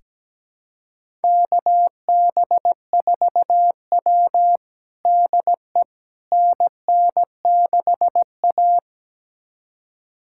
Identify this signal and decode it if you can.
Morse code, heard as a keyed tone, about 17 wpm, 700 Hz. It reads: KB4W DE NN6A
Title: USA Callsign DE Callsign - 4 Characters - 17wpm